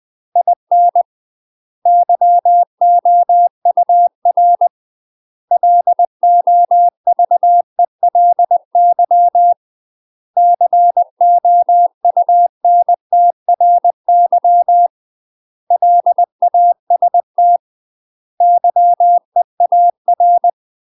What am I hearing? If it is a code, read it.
Morse code, heard as a keyed tone, about 20 wpm, 700 Hz. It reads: IN YOUR LOVELY COUNTRY LAST YEAR
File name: UK MASTER QSOs TRAINING / 20WPM / UK MASTER QSOs 0021